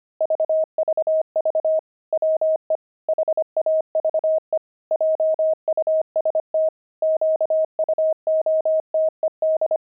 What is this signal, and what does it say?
Morse code, heard as a keyed tone, about 25 wpm, 630 Hz. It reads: VVV WE HAVE JUST QUOTED
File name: pangram-fs-25-cs-25-nine